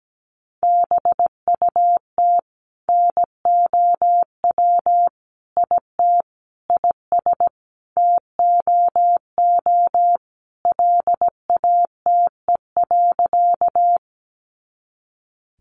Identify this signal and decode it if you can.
Morse code, heard as a keyed tone, about 17 wpm, 700 Hz. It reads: BUT NOW IT IS TOO LATE.
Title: Sentences from Top 300 Words 17wpm